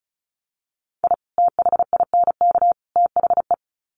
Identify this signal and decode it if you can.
Morse code, heard as a keyed tone, about 34 wpm, 700 Hz. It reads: I THINK THE
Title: Sets of 3 Words 35wpm